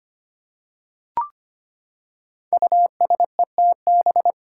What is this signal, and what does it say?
Morse code, heard as a keyed tone, about 25 wpm, 700 Hz. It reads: USETB
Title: Five Challenging Letter-Number Combinations 25wpm